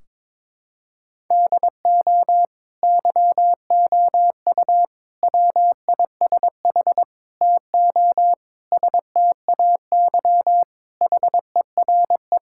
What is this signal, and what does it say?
Morse code, heard as a keyed tone, about 21 wpm, 700 Hz. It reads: DO YOU WISH TO STAY HERE
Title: Sentences from Top 1600 Words 22wpm